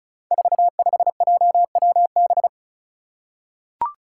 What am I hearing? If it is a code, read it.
Morse code, heard as a keyed tone, about 34 wpm, 700 Hz. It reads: 45JWB
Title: Five Challenging Letter-Number Combinations 35wpm